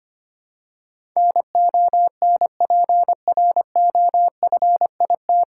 Morse code, heard as a keyed tone, about 25 wpm, 700 Hz. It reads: NONPROFIT
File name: practice / 25wpm / 9 Letter Words - First 3 Letters Given - 25wpm